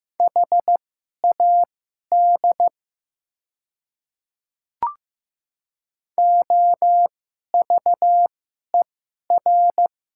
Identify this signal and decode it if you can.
Morse code, heard as a keyed tone, about 15 wpm, spaced as for 11 wpm, 700 Hz. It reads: HAD OVER
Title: 2x Character Spacing - Top 200 Words - 15wpm